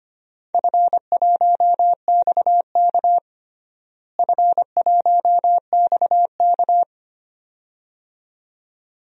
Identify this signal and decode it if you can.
Morse code, heard as a keyed tone, about 25 wpm, 700 Hz. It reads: F1XK F1XK
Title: DXCC-100 Call Signs - 4 Characters - Repeated 2x - 25wpm